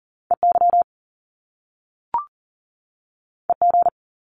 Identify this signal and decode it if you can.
Morse code, heard as a keyed tone, about 38 wpm, 700 Hz. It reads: EY EG
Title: Top Two Letter Combinations in Words 40wpm